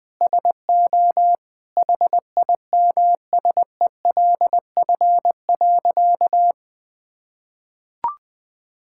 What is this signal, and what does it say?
Morse code, heard as a keyed tone, about 20 wpm, 700 Hz. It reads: SO HIMSELF.
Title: Sentences from Top 1600 Words 20wpm